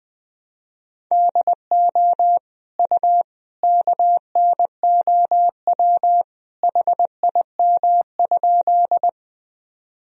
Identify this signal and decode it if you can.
Morse code, heard as a keyed tone, about 20 wpm, 700 Hz. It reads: DO U KNOW HIM?
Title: Sentences from Top 100 Words and Abbreviations 20wpm